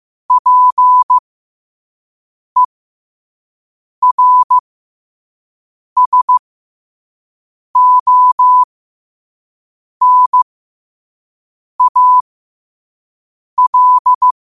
Morse code, heard as a keyed tone, about 15 wpm, spaced as for 5 wpm, 1000 Hz. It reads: PERSONAL
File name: qotdfiles / 5wpm